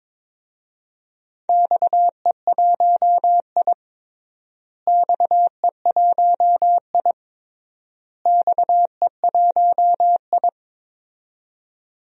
Morse code, heard as a keyed tone, about 22 wpm, 700 Hz. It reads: XE1I XE1I XE1I
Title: DXCC-100 Call Signs - 4 Characters - Repeated 3x - 22wpm